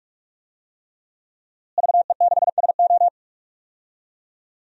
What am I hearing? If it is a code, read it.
Morse code, heard as a keyed tone, about 46 wpm, 700 Hz. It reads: VE6SO
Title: DXCC-100 Call Signs - 5 Characters - Repeated 3x - 45wpm